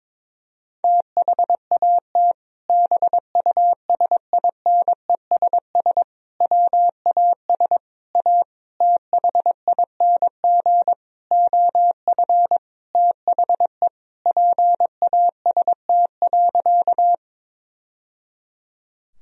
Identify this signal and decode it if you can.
Morse code, heard as a keyed tone, about 22 wpm, 700 Hz. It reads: THAT BUSINESS WAS A THING OF THE PAST.
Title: Sentences from Top 1000 Words - Review Part 3 - 22wpm